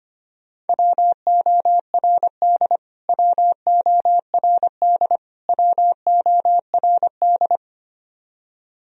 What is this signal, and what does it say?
Morse code, heard as a keyed tone, about 25 wpm, 700 Hz. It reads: WORD WORD WORD
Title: Top 100 Words Repeated 25wpm